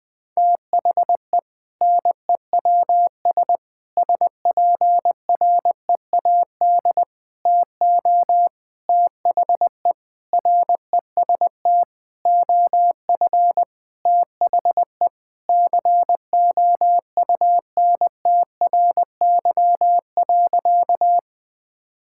Morse code, heard as a keyed tone, about 20 wpm, 700 Hz. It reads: THE NEWS SPREAD TO THE REST OF THE COUNTRY.
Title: Sentences with 5-Word Phrases - Spoken Beginning - 20wpm